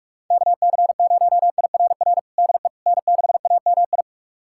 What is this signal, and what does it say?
Morse code, heard as a keyed tone, about 43 wpm, 700 Hz. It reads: KC0IRR DE N6AMI
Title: USA Callsign DE Callsign 45wpm